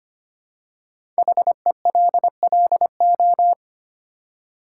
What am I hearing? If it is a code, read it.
Morse code, heard as a keyed tone, about 25 wpm, 700 Hz. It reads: HELLO
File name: 5 Letter Words - First 3 Letters Given - 25wpm